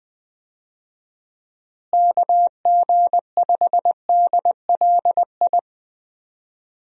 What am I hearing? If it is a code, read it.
Morse code, heard as a keyed tone, about 19 wpm, 700 Hz. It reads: KG5DLI
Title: USA Call Signs - 6 Characters - Repeated 3x - 20wpm